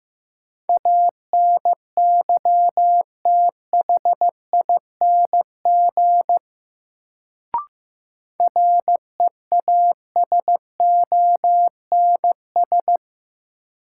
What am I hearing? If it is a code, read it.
Morse code, heard as a keyed tone, about 15 wpm, 700 Hz. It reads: ANYTHING REASONS
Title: Sets of 4 Words - New Word Drill - 15wpm